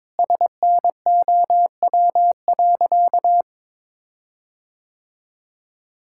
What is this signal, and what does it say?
Morse code, heard as a keyed tone, about 22 wpm, 700 Hz. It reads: SNOW.
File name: practice / 22wpm / Sentences with 8-Word Phrases - 22wpm